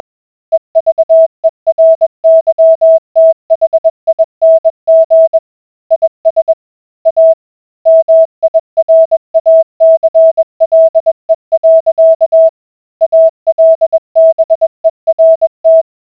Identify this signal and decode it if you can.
Morse code, heard as a keyed tone, about 20 wpm, 650 Hz. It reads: EVERYTHING IS A MIRACLE. ALBERT